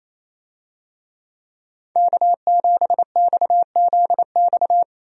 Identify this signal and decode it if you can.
Morse code, heard as a keyed tone, about 28 wpm, 700 Hz. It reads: K7XZX
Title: USA Call Signs - 5 Characters - 28wpm